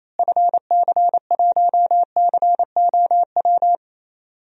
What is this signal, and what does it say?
Morse code, heard as a keyed tone, about 28 wpm, 700 Hz. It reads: FC1COW
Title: DXCC-100 Call Signs - 6 Characters - Repeated 3x - 28wpm